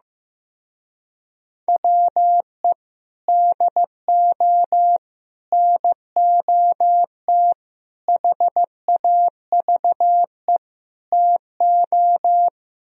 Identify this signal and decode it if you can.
Morse code, heard as a keyed tone, about 15 wpm, 700 Hz. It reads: WE DO NOT HAVE TO
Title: Speed-Racing - Sets of 5 Words 15wpm